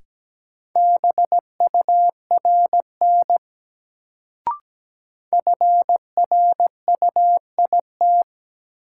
Morse code, heard as a keyed tone, about 17 wpm, 700 Hz. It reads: BURN FRUIT
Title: Sentences from Top 1300 Words - New Word Drill Alt - 17wpm